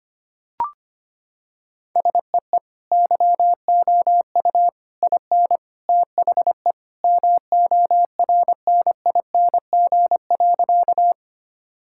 Morse code, heard as a keyed tone, about 25 wpm, 700 Hz. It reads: SEE YOU IN THE MORNING.
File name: practice / 25wpm / Sentences with 3-Word Phrases - 25wpm